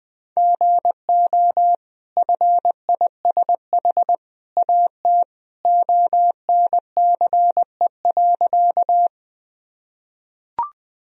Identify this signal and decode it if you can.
Morse code, heard as a keyed tone, about 20 wpm, 700 Hz. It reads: GO FISH AT ONCE.
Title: Sentences from Top 1000 Words - Review Part 1 - 20wpm